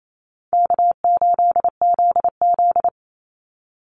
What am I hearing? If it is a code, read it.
Morse code, heard as a keyed tone, about 28 wpm, 700 Hz. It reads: K8ZZ
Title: Easy Call Signs Repeated 3x - 28wpm